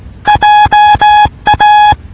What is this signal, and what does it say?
Morse code, heard as a keyed tone, about 18 wpm, 840 Hz. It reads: JA